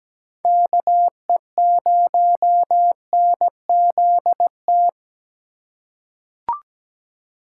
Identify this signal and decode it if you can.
Morse code, heard as a keyed tone, about 17 wpm, 700 Hz. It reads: KE0NZT
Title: USA Call Signs - 6 Characters - 4x Character Spacing - 17wpm